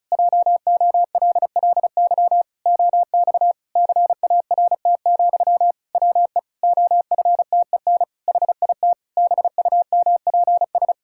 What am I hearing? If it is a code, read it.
Morse code, heard as a keyed tone, about 36 wpm, 700 Hz. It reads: JOLLY OX CART, WE OFTEN HIT BUMPS
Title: Pangram Challenge v2 - 35wpm